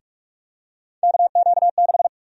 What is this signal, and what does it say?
Morse code, heard as a keyed tone, about 46 wpm, 700 Hz. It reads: KQ6
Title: USA Call Sign Prefixes - 45wpm